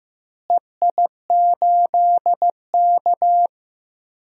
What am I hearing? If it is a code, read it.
Morse code, heard as a keyed tone, about 15 wpm, 700 Hz. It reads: EI8K
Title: DXCC-100 Call Signs - 4 Characters - Repeated 3x - 15wpm